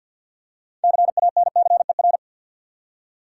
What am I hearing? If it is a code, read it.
Morse code, heard as a keyed tone, about 50 wpm, 700 Hz. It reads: CANCER
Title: Six Letter Words - 50wpm